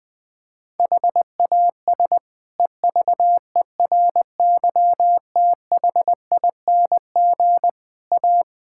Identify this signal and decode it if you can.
Morse code, heard as a keyed tone, about 21 wpm, 700 Hz. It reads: HAS EVERYTHING A
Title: Sentences from Top 1600 Words 20wpm